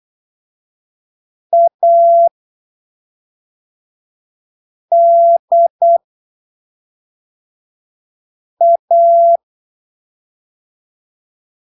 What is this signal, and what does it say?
Morse code, heard as a keyed tone, about 8 wpm, spaced as for 3 wpm, 680 Hz. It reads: ADA
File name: arquivos / texto / texto 10 08ppm